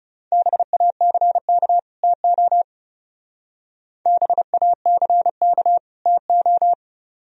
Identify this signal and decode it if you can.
Morse code, heard as a keyed tone, about 35 wpm, 700 Hz. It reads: BACK TO BACK TO
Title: Speed-Racing - Sets of 2 Words 30wpm